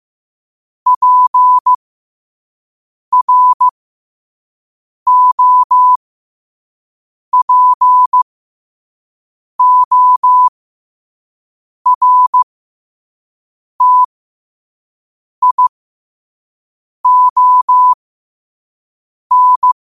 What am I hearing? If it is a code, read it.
Morse code, heard as a keyed tone, about 15 wpm, spaced as for 5 wpm, 1000 Hz. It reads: PROPORTION